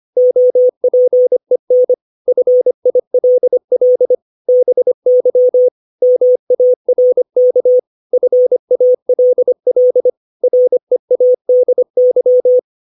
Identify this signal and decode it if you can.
Morse code, heard as a keyed tone, about 25 wpm, 500 Hz. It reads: OPEN FILL BY MARK FALL READY
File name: 632 words / 632 Words 25WPM